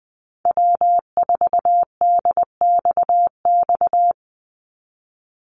Morse code, heard as a keyed tone, about 20 wpm, 700 Hz. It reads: W4DXX
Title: Call Signs Repeated 2x - 20wpm